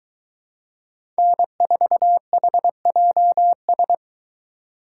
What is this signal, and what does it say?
Morse code, heard as a keyed tone, about 23 wpm, 700 Hz. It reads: N4HJS